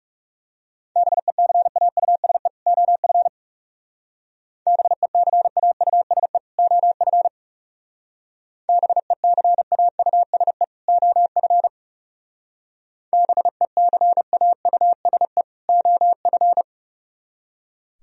Morse code, heard as a keyed tone, about 35 wpm, 700 Hz. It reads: BECAUSE OF BECAUSE OF BECAUSE OF BECAUSE OF
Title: Speed-Racing - Sets of 2 Words 30wpm